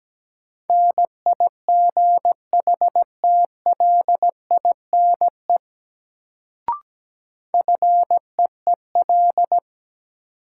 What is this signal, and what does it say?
Morse code, heard as a keyed tone, about 17 wpm, 700 Hz. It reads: NIGHTLINE FEEL
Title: Sets of 5 Words - New Word Drill - 17wpm